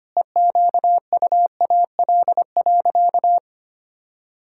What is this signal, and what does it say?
Morse code, heard as a keyed tone, about 25 wpm, 700 Hz. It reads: EQUAL.
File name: practice / 25wpm / Sentences with 9-Word Phrases - 25wpm